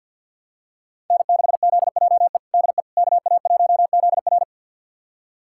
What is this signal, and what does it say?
Morse code, heard as a keyed tone, about 53 wpm, 700 Hz. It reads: N6ZJE DE KA1ZR
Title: USA Callsign DE Callsign - 5 Characters - 50wpm